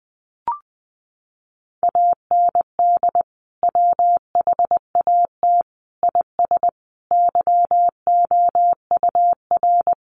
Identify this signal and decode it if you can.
Morse code, heard as a keyed tone, about 20 wpm, 700 Hz. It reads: AND WHAT IS YOUR
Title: Sentences from Top 1200 Words 20wpm